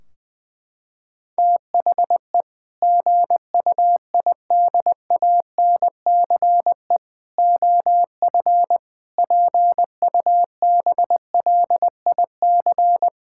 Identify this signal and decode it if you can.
Morse code, heard as a keyed tone, about 20 wpm, 700 Hz. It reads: THE GUIDANCE OF PUBLIC
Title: Sentences from Top 1600 Words 20wpm